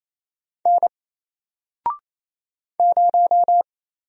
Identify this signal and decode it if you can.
Morse code, heard as a keyed tone, about 27 wpm, 700 Hz. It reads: N 0
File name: Single-Letter-Number-28wpm